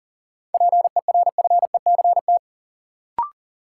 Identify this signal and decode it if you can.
Morse code, heard as a keyed tone, about 40 wpm, 700 Hz. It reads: PERFECT
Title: Sentences from Top 1000 Words - New Word Drill - 40wpm